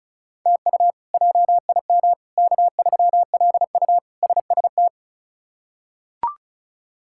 Morse code, heard as a keyed tone, about 34 wpm, 700 Hz. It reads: TU JIM K3LU SST
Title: K1USN SST QSO Practice - 35wpm